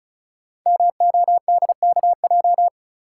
Morse code, heard as a keyed tone, about 36 wpm, 700 Hz. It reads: MODKJ
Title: Code Groups - 5 Letter:Numbers - 35wpm